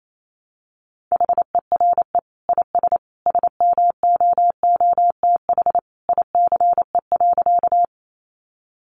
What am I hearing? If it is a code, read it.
Morse code, heard as a keyed tone, about 28 wpm, 700 Hz. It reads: HERE IS SMOOTH ICE.
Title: Sentences from Top 1400 Words 28wpm